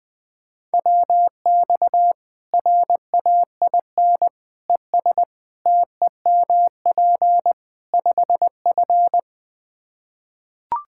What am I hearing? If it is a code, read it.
Morse code, heard as a keyed tone, about 21 wpm, 700 Hz. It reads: WX RAIN ES TEMP 5F
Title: K1USN SST QSO Practice 20wpm